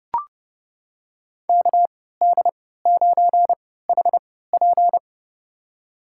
Morse code, heard as a keyed tone, about 30 wpm, spaced as for 17 wpm, 700 Hz. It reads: KD9HP
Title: USA Call Signs - 5 Characters - 3x Character Spacing - 30wpm